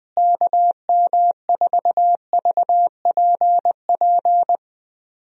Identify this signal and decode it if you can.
Morse code, heard as a keyed tone, about 20 wpm, 700 Hz. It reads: KM4VPP
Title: USA Call Signs - 6 Characters - Repeated 3x - 20wpm